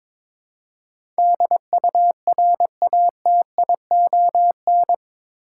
Morse code, heard as a keyed tone, about 22 wpm, 700 Hz. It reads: DURATION
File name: practice / 22wpm / Eight Letter Words - 22wpm